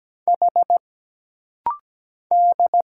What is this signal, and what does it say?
Morse code, heard as a keyed tone, about 17 wpm, 700 Hz. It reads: H D
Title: Single Letters - Rapid-Fire 17wpm